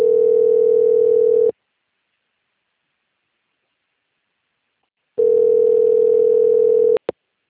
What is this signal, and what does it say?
Ringback tone (North America), 440 + 480 Hz.
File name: out- - - - - .42817